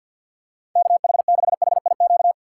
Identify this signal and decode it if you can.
Morse code, heard as a keyed tone, about 52 wpm, 700 Hz. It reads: KH6HIQ
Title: USA Call Signs - 6 Characters - 4x Character Spacing - 50wpm